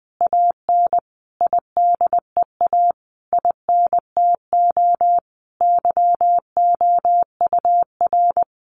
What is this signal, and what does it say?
Morse code, heard as a keyed tone, about 20 wpm, 700 Hz. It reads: AN IDEA INTO YOUR
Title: Sentences from Top 600 Words 20wpm